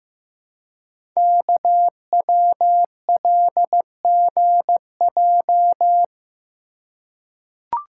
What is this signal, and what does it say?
Morse code, heard as a keyed tone, about 15 wpm, 700 Hz. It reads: KWLGJ
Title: Code Groups - 5 Letters - 15wpm